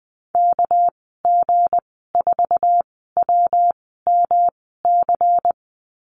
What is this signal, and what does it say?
Morse code, heard as a keyed tone, about 19 wpm, spaced as for 14 wpm, 700 Hz. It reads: KG4WMC
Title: 238 Lesson 38g - Letter q - Call Signs 20wpm